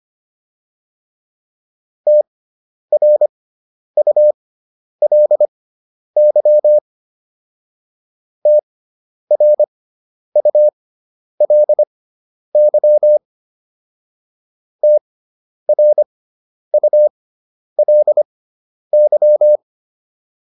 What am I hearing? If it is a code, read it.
Morse code, heard as a keyed tone, about 25 wpm, spaced as for 10 wpm, 600 Hz. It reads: TRULY TRULY TRULY